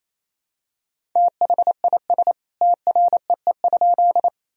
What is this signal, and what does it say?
Morse code, heard as a keyed tone, about 28 wpm, 700 Hz. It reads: THIS TREE?
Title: Sentences from Top 1000 Words - Review Part 1 - 28wpm